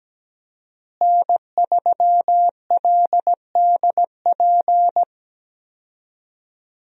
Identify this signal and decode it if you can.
Morse code, heard as a keyed tone, about 17 wpm, 700 Hz. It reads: N3LDP
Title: USA Call Signs - 5 Characters - Repeated 3x - 17wpm